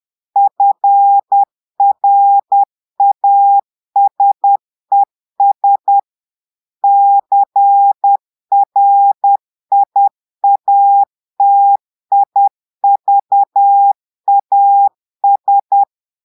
Morse code, heard as a keyed tone, about 10 wpm, 800 Hz. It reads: FRASES CRIATIVAS
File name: portugais texte en clair 2 vitesse 10 mots minutes